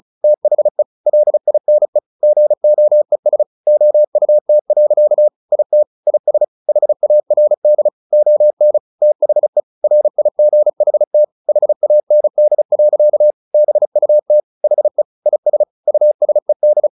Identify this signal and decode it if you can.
Morse code, heard as a keyed tone, about 34 wpm, 600 Hz. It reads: THE LINE GOES OUT. IT IS HARD ON THE RIGHT HAND. BUT HE IS USED
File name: the old man and the sea 2 35wpm